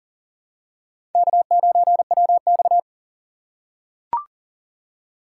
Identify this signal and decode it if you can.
Morse code, heard as a keyed tone, about 40 wpm, 700 Hz. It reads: K9WX
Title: USA Call Signs - 6x Character Spacing - 40wpm